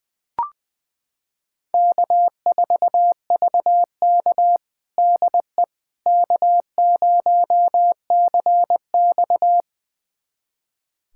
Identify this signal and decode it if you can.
Morse code, heard as a keyed tone, about 20 wpm, 700 Hz. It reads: K4VK DE K0CX
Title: USA Callsign DE Callsign - 4 Characters - 20wpm